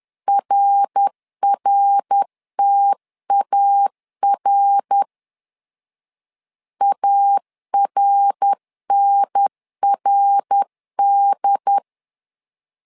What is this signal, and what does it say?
Morse code, heard as a keyed tone, about 10 wpm, 800 Hz. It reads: RRTAR ARNRD
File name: LEZIONI / 3 MWRO / 3 MWR 10wpm 1952